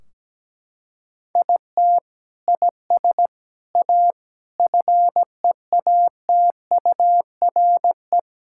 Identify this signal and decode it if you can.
Morse code, heard as a keyed tone, about 17 wpm, 700 Hz. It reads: IT IS A FEATURE
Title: Sentences from Top 1600 Words 17wpm